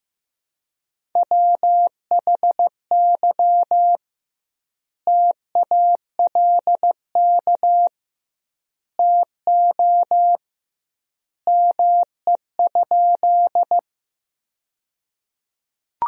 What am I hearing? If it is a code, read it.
Morse code, heard as a keyed tone, about 14 wpm, 700 Hz. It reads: WHY TALK TO ME?
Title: 2x Word-Spacing - Sentences from Top 400 Words - 15wpm